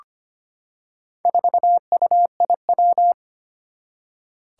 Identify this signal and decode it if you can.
Morse code, heard as a keyed tone, about 25 wpm, 700 Hz. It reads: 4UIW
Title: Four Challenging Letter-Number Combinations 25wpm